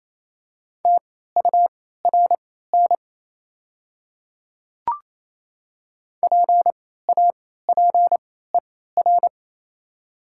Morse code, heard as a keyed tone, about 28 wpm, spaced as for 16 wpm, 700 Hz. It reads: TURN PAPER
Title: 3x Character Spacing - Top 300 Words - 28wpm